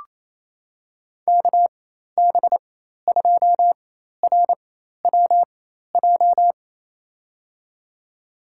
Morse code, heard as a keyed tone, about 27 wpm, spaced as for 13 wpm, 700 Hz. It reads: KB2RWJ
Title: USA Call Signs - 6 Characters - 4x Character Spacing - 28wpm